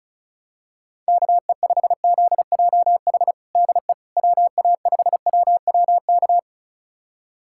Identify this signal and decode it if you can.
Morse code, heard as a keyed tone, about 36 wpm, 700 Hz. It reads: KE5ZJH DE WA5WWK
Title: USA Callsign DE Callsign - 6 Characters - 35wpm